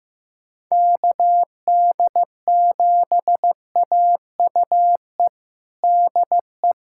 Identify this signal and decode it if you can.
Morse code, heard as a keyed tone, about 15 wpm, 700 Hz. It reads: KD7AUE DE
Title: USA Callsign DE Callsign - 6 Characters - 15wpm